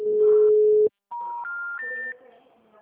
Special information tone.
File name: out- -2003